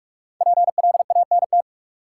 Morse code, heard as a keyed tone, about 47 wpm, 700 Hz. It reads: PLANT
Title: Sentences from Top 1000 Words - Review Part 1 - New Word Drill - 45wpm